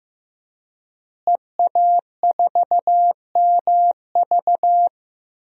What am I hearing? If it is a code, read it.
Morse code, heard as a keyed tone, about 15 wpm, 700 Hz. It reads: EA4MV
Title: DXCC-100 Call Signs - 5 Characters - Repeated 3x - 15wpm